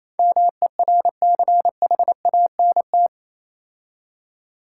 Morse code, heard as a keyed tone, about 27 wpm, 700 Hz. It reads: MERCHANT
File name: Easy Words - Speed-Racing with Hints - 22wpm